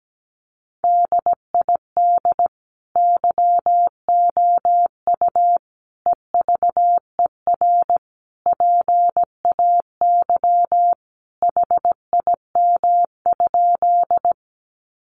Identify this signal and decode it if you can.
Morse code, heard as a keyed tone, about 17 wpm, 700 Hz. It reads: DID YOU EVER PAY HIM?
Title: Sentences from Top 700 Words 17wpm